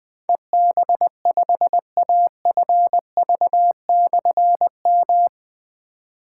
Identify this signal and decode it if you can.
Morse code, heard as a keyed tone, about 20 wpm, 700 Hz. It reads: EB5AFV/M
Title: Difficult Call Signs 20wpm